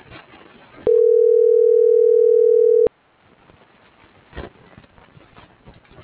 Ringback tone (North America), 440 + 480 Hz.